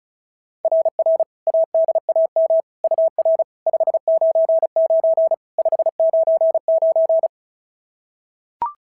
Morse code, heard as a keyed tone, about 36 wpm, 690 Hz. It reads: RR ADAM UR 599 599
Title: SOTA QSO Practice 35wpm